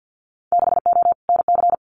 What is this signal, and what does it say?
Morse code, heard as a keyed tone, about 50 wpm, 700 Hz. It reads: 6O NC
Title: Winter Field Day Exchanges 50wpm